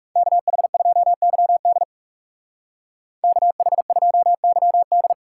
Morse code, heard as a keyed tone, about 44 wpm, 700 Hz. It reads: KH2YD KH2YD